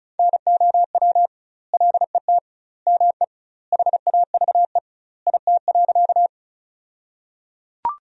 Morse code, heard as a keyed tone, about 34 wpm, 700 Hz. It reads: NOW LET ME HAVE IT.
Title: 2x Word-Spacing - Sentences from Top 300 Words - 35wpm